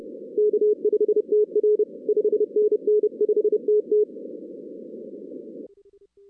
Morse code, heard as a keyed tone, about 30 wpm, 410 Hz. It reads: K5TR 5NN 5TT
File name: 2016 arrl dx cw / 2016-ARRL-DX-CW-K5TR- - 0-46